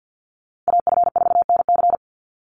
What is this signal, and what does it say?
Morse code, heard as a keyed tone, about 48 wpm, 700 Hz. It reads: AF4NC
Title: Call Signs Repeated 3x - 50wpm